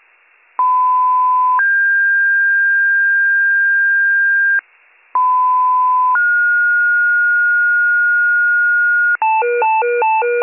Two-tone page, heard as a one-second tone, then a three-second tone, about 1010 then 1600 Hz, about 1010 then 1430 Hz, with noise. It is heard from 0.6 to 9.2 s.